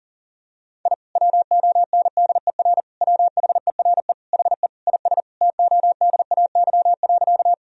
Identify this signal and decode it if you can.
Morse code, heard as a keyed tone, about 40 wpm, 700 Hz. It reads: I WONDER WHERE HE IS TODAY.
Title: Sentences from Top 1000 Words - Review Part 3 - 40wpm